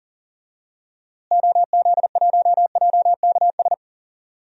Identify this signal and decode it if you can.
Morse code, heard as a keyed tone, about 40 wpm, 700 Hz. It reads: OZ1JKS
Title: DXCC-100 Call Signs - 6 Characters - Repeated 2x - 40wpm